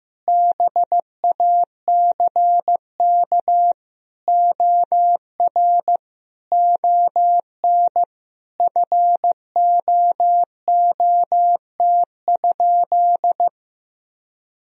Morse code, heard as a keyed tone, about 15 wpm, 700 Hz. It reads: BACK OR ON FOOT?
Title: Sentences from Top 1000 Words - Review Part 3 - 15wpm